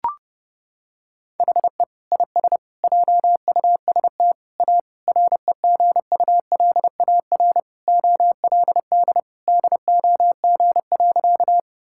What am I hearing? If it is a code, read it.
Morse code, heard as a keyed tone, about 30 wpm, 700 Hz. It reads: HE IS JUST A REGULAR OLD DOG.
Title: Sentences from Top 1000 Words - Review Part 6 - 30wpm